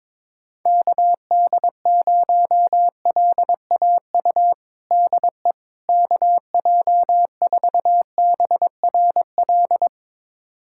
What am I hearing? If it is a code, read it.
Morse code, heard as a keyed tone, about 22 wpm, 700 Hz. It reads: KD0LAU DE KJ4BRL
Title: USA Callsign DE Callsign 22wpm